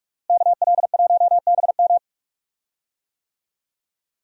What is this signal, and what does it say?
Morse code, heard as a keyed tone, about 43 wpm, 700 Hz. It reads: KL1BM
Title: USA Call Signs - 5 Characters - Repeated 3x - 45wpm